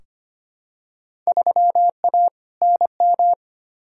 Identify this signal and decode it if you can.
Morse code, heard as a keyed tone, about 25 wpm, 700 Hz. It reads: 3A NM
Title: Field Day Exchanges Encore - 25wpm